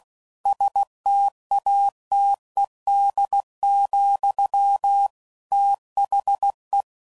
Morse code, heard as a keyed tone, about 16 wpm, 800 Hz. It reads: STATED, THE
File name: CW Code Groups @15WPM / Technical Subjects / XL